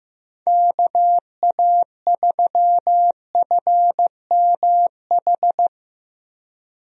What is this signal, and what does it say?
Morse code, heard as a keyed tone, about 15 wpm, 700 Hz. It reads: KA3FMH
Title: USA Call Signs - 6 Characters - Repeated 2x - 15wpm